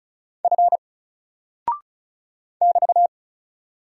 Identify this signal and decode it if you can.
Morse code, heard as a keyed tone, about 36 wpm, 700 Hz. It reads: F <BT>
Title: Single Character (Letter:Number:Punctuation:Prosign) Alt 35wpm